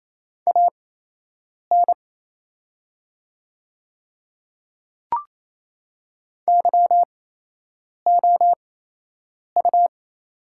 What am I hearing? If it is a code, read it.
Morse code, heard as a keyed tone, about 28 wpm, spaced as for 8 wpm, 700 Hz. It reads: AN YOU
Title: 8x Character Spacing - Top 100 Words - 28wpm